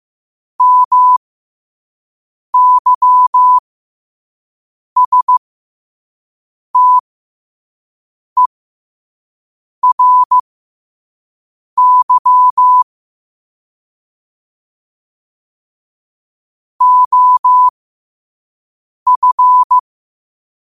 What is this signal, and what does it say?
Morse code, heard as a keyed tone, about 15 wpm, spaced as for 5 wpm, 1000 Hz. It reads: MYSTERY OF
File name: qotdfiles / 5wpm